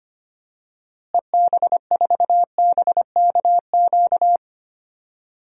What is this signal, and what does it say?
Morse code, heard as a keyed tone, about 25 wpm, 700 Hz. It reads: EB4BKQ
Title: DXCC-100 Call Signs - 6 Characters - Repeated 2x - 25wpm